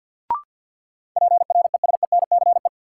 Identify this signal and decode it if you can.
Morse code, heard as a keyed tone, about 52 wpm, 700 Hz. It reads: PRESENCE
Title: Sets of 3 Words - New Word Drill - 50wpm